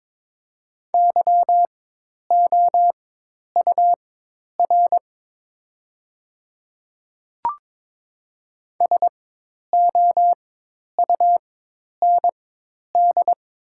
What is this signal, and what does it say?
Morse code, heard as a keyed tone, about 21 wpm, spaced as for 10 wpm, 700 Hz. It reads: YOUR SOUND
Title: 4x Character Spacing - Top 100 Words - 22wpm